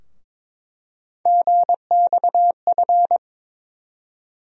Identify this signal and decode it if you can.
Morse code, heard as a keyed tone, about 22 wpm, 700 Hz. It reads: GXF
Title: USA Call Sign Postfixes - 3 Letters 22wpm